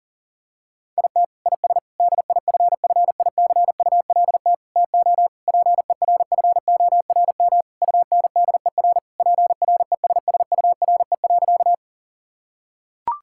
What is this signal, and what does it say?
Morse code, heard as a keyed tone, about 37 wpm, 700 Hz. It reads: IT IS DIFFICULT TO PERFORM UNDER PRESSURE.
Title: Sentences from Top 1000 Words - Review Part 6 - 40wpm